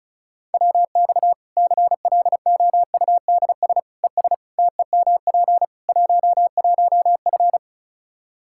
Morse code, heard as a keyed tone, about 36 wpm, 700 Hz. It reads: WX CLOUDS ES TEMP 11F
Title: Practice QSO - Exchange 2 - Info - 35wpm